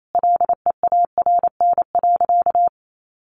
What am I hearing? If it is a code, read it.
Morse code, heard as a keyed tone, about 28 wpm, 700 Hz. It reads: LEARN.
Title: Sentences from Top 800 Words 28wpm